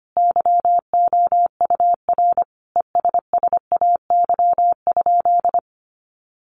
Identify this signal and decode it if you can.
Morse code, heard as a keyed tone, about 25 wpm, 700 Hz. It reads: YOUR ESSAY?
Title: Sentences from Top 1500 Words 25wpm